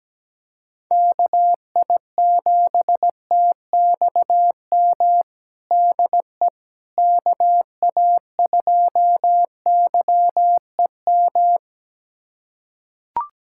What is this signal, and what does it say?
Morse code, heard as a keyed tone, about 17 wpm, 700 Hz. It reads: KI7TXM DE KA2YEM
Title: USA Callsign DE Callsign - 6 Characters - 17wpm